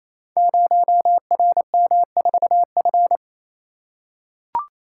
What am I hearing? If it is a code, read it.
Morse code, heard as a keyed tone, about 29 wpm, 700 Hz. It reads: 0RM4F
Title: Code Groups - 5 Letter:Numbers - 28wpm